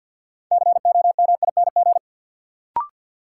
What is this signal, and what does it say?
Morse code, heard as a keyed tone, about 53 wpm, 700 Hz. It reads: COMING
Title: Sentences from Top 1300 Words - New Word Drill Alt - 50wpm